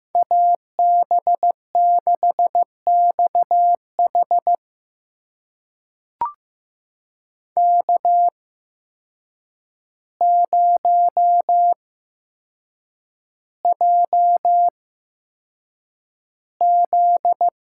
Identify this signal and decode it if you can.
Morse code, heard as a keyed tone, about 15 wpm, spaced as for 7 wpm, 700 Hz. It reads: AB6XH K0JZ
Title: USA Call Signs - 8x Character Spacing - 15wpm